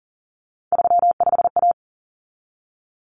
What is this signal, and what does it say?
Morse code, heard as a keyed tone, about 40 wpm, 700 Hz. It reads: 35A
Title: Common Cut Numbers - 3-Digit 40wpm